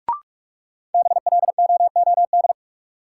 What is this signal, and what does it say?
Morse code, heard as a keyed tone, about 46 wpm, 700 Hz. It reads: BLOOD
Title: Sentences from Top 1000 Words - Review Part 2 - New Word Drill - 45wpm